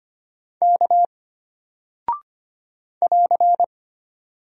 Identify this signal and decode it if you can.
Morse code, heard as a keyed tone, about 25 wpm, 700 Hz. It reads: K <AR>
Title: Single Character (Letter:Number:Punctuation:Prosign) Alt 25wpm